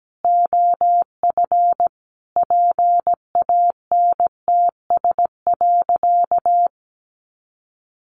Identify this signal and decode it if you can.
Morse code, heard as a keyed tone, about 17 wpm, 700 Hz. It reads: OF PANTS.
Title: Additional Easy Phrases 17wpm